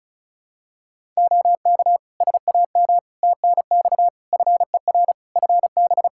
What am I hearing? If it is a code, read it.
Morse code, heard as a keyed tone, about 36 wpm, 700 Hz. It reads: OK SAM TNX FER FB
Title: Practice QSO - Exchange 4 - Ending - 35wpm